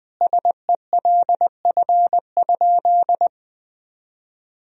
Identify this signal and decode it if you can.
Morse code, heard as a keyed tone, about 21 wpm, 700 Hz. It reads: SELF?
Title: Sentences from Top 1000 Words - Review Part 1 - 20wpm